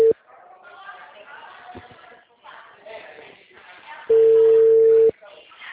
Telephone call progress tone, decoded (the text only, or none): ringback tone